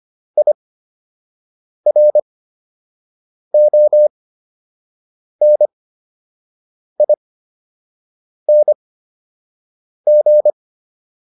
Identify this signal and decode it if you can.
Morse code, heard as a keyed tone, about 25 wpm, spaced as for 6 wpm, 600 Hz. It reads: IRONING